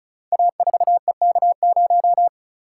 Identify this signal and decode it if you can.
Morse code, heard as a keyed tone, about 36 wpm, 700 Hz. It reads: A4EK0
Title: Code Groups - 5 Letter:Numbers - 35wpm